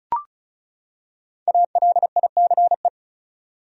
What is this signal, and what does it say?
Morse code, heard as a keyed tone, about 36 wpm, 700 Hz. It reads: ALICE
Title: USA Common Names - 35wpm